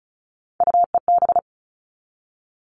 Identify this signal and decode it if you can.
Morse code, heard as a keyed tone, about 34 wpm, 700 Hz. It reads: UEB
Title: All Cut Numbers - 3-Digit 35wpm